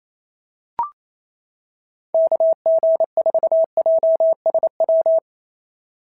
Morse code, heard as a keyed tone, about 25 wpm, 650 Hz. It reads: KG4JSW
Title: SOTA QSO Practice 28wpm